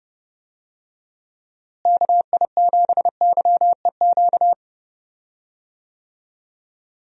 Morse code, heard as a keyed tone, about 30 wpm, 700 Hz. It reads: KI7YEQ
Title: USA Call Signs - 6 Characters - Repeated 3x - 30wpm